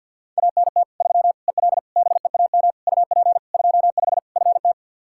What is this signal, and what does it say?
Morse code, heard as a keyed tone, about 50 wpm, 700 Hz. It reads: ANT 3 EL BEAM UP 25 FT
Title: Practice QSO - Exchange 2 - Info - 50wpm